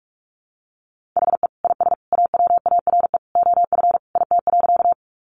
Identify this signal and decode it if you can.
Morse code, heard as a keyed tone, about 44 wpm, 700 Hz. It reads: HE IS AWARE OF IT.
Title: Sentences from Top 1200 Words 45wpm